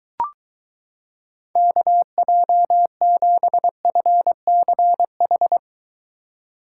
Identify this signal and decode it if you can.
Morse code, heard as a keyed tone, about 23 wpm, 700 Hz. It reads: KJ7FCH